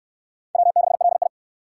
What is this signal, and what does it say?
Morse code, heard as a keyed tone, about 62 wpm, 700 Hz. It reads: W6LI